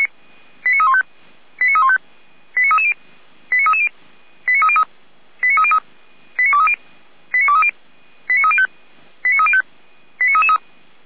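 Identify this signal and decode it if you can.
Five-tone selective call (ZVEI), heard as quick runs of five tones, tones of about 70 ms. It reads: 89315 89315 89330 89330 89303 89303 89230 89230 89305 89305 89333
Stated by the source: ZVEI five-tone call received by SDR.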